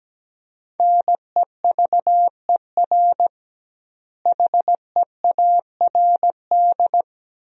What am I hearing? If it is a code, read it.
Morse code, heard as a keyed tone, about 17 wpm, 700 Hz. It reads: NEVER HEARD
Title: 2x Word-Spacing - Sentences from Top 400 Words - 17wpm